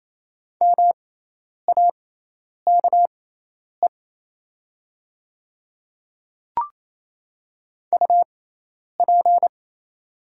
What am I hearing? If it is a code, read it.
Morse code, heard as a keyed tone, about 28 wpm, spaced as for 10 wpm, 690 Hz. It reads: MAKE UP
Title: 6x Character Spacing - Top 500 Words - 28wpm